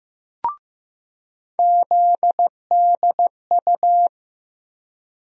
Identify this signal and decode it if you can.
Morse code, heard as a keyed tone, about 15 wpm, 700 Hz. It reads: ZDU
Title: USA Call Sign Postfixes - 3 Letters 15wpm